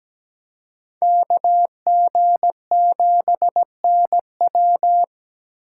Morse code, heard as a keyed tone, about 17 wpm, 700 Hz. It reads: KG7NW
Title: USA Call Signs - 5 Characters - 3x Character Spacing - 17wpm